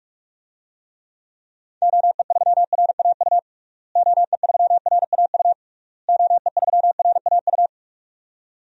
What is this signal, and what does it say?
Morse code, heard as a keyed tone, about 46 wpm, 700 Hz. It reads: OE3RAU OE3RAU OE3RAU
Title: DXCC-100 Call Signs - 6 Characters - Repeated 3x - 45wpm